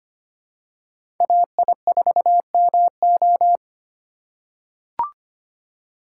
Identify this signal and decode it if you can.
Morse code, heard as a keyed tone, about 25 wpm, 700 Hz. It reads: AI4MO
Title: USA Call Signs - 3x Character Spacing - 25wpm